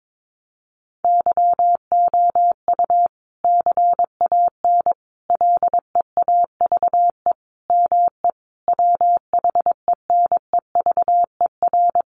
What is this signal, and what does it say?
Morse code, heard as a keyed tone, about 22 wpm, 700 Hz. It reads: YOU CAN LEAVE ME WHENEVER
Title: Sentences from Top 1200 Words 22wpm